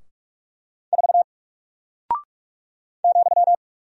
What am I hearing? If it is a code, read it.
Morse code, heard as a keyed tone, about 46 wpm, 700 Hz. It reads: 4 ,
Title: Single Character (Letter:Number:Punctuation:Prosign) Alt 45wpm